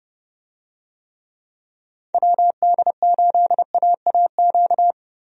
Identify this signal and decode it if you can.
Morse code, heard as a keyed tone, about 32 wpm, 700 Hz. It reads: WD8AAQ
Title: USA Call Signs - 6 Characters - 30wpm